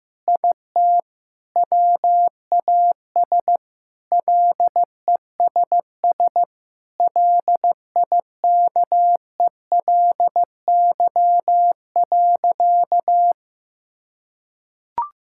Morse code, heard as a keyed tone, about 15 wpm, 700 Hz. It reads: IT WAS LESS LIKELY.
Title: Sentences from Top 1600 Words 15wpm